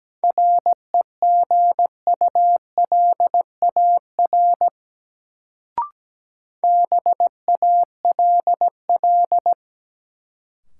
Morse code, heard as a keyed tone, about 17 wpm, 700 Hz. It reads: REGULAR BALL
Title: Sentences from Top 900 Words - New Word Drill - 17wpm